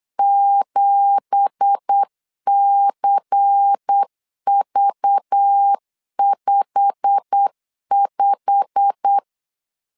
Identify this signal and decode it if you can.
Morse code, heard as a keyed tone, about 8 wpm, 800 Hz. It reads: 7CV55
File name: CW / DET / 1 9 1500C S C 8wpm 5653